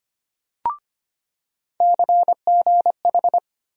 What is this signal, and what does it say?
Morse code, heard as a keyed tone, about 25 wpm, 700 Hz. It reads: CGH
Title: USA Call Sign Postfixes - 3 Letters 25wpm